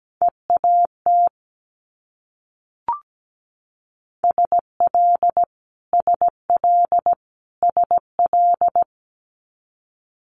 Morse code, heard as a keyed tone, about 17 wpm, 700 Hz. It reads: EAT SL SL SL
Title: Multi-letter Phonemes Repeated 3x 17wpm